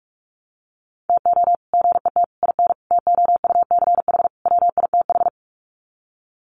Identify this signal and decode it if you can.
Morse code, heard as a keyed tone, about 44 wpm, 700 Hz. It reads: TO GET IN TOUCH WITH
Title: Sets of 5 Words Encore2 45wpm